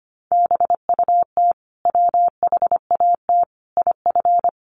Morse code, heard as a keyed tone, about 25 wpm, 700 Hz. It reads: BUT WHAT IF
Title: Sentences from Top 1400 Words 25wpm